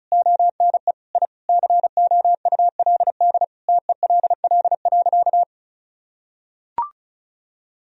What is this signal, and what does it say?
Morse code, heard as a keyed tone, about 34 wpm, 700 Hz. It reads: ONE I COULD TELL.
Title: Sentences from Top 1000 Words - Review Part 4 - 35wpm